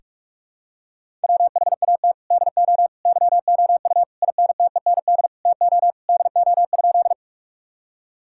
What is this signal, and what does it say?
Morse code, heard as a keyed tone, about 46 wpm, 700 Hz. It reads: WHAT DO YOU INTEND TO DO?
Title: Sentences from Top 1600 Words 45wpm